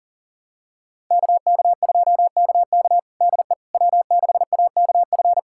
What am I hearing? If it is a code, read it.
Morse code, heard as a keyed tone, about 40 wpm, 700 Hz. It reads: KK2KK DE W6AKF
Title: USA Callsign DE Callsign - 5 Characters - 40wpm